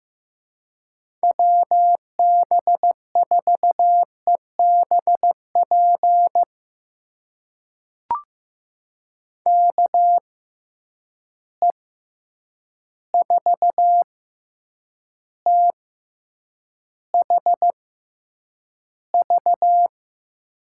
Morse code, heard as a keyed tone, about 15 wpm, spaced as for 8 wpm, 700 Hz. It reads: WB4EBP KE4THV
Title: USA Call Signs - 6 Characters - 6x Character Spacing - 15wpm